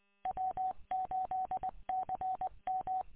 Morse code, heard as a keyed tone, about 21 wpm, 750 Hz. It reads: W8CM
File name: w8cmn- -combined